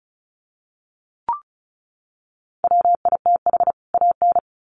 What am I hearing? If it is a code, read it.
Morse code, heard as a keyed tone, about 34 wpm, 700 Hz. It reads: WITH AN
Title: Sets of 2 Words 35wpm